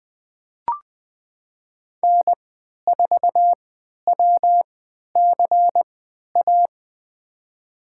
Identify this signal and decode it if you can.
Morse code, heard as a keyed tone, about 20 wpm, spaced as for 11 wpm, 700 Hz. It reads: N4WCA
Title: USA Call Signs - 3x Character Spacing - 20wpm